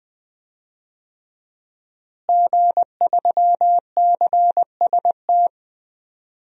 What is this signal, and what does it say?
Morse code, heard as a keyed tone, about 20 wpm, 700 Hz. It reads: G3CST
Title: DXCC-100 Call Signs - 5 Characters - Repeated 3x - 20wpm